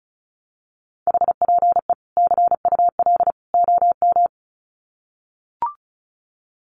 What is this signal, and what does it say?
Morse code, heard as a keyed tone, about 34 wpm, 700 Hz. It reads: HPE CUL OM
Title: Common QSO Phrases 35wpm